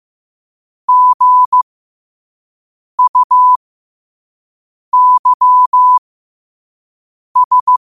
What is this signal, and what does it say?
Morse code, heard as a keyed tone, about 15 wpm, spaced as for 5 wpm, 1000 Hz. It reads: GUYS
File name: 5wpm